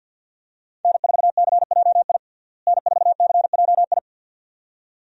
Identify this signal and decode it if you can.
Morse code, heard as a keyed tone, about 48 wpm, 700 Hz. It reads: N4CJI N4CJI
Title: USA Call Signs - 5 Characters - Repeated 2x - 50wpm